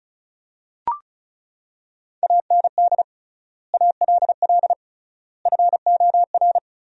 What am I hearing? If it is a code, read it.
Morse code, heard as a keyed tone, about 34 wpm, 700 Hz. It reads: AND ALL FOR
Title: 3x Word-Spacing - Sentences from Top 100 Words - 35wpm